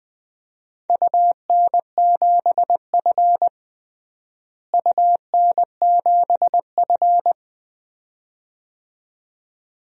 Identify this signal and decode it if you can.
Morse code, heard as a keyed tone, about 21 wpm, 700 Hz. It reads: UN7F UN7F
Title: DXCC-100 Call Signs - 4 Characters - Repeated 2x - 20wpm